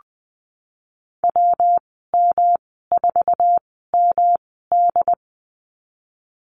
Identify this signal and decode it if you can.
Morse code, heard as a keyed tone, about 19 wpm, spaced as for 14 wpm, 700 Hz. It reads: WM4MD
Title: 128 Lesson 21g - Number 6 - Call Signs 20wpm